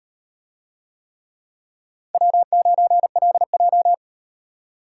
Morse code, heard as a keyed tone, about 38 wpm, 700 Hz. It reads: W9LJ